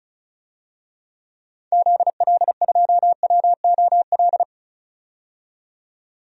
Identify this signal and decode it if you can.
Morse code, heard as a keyed tone, about 36 wpm, 700 Hz. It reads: ZL2WOL
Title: DXCC-100 Call Signs - 6 Characters - 35wpm